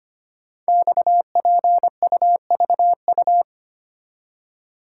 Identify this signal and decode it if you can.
Morse code, heard as a keyed tone, about 25 wpm, 700 Hz. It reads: XPUVU
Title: Code Groups - 5 Letters - 25wpm